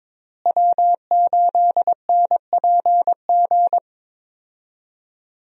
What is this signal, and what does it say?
Morse code, heard as a keyed tone, about 21 wpm, 700 Hz. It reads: W8NPG
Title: USA Call Signs - 5 Characters - Repeated 3x - 22wpm